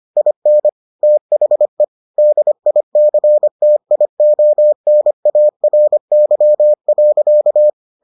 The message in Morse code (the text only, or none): IN THE DICTIONARY.